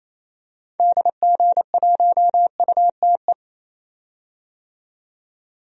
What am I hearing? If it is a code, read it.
Morse code, heard as a keyed tone, about 28 wpm, 700 Hz. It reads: DG1UTE
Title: DXCC-100 Call Signs - 6 Characters - 28wpm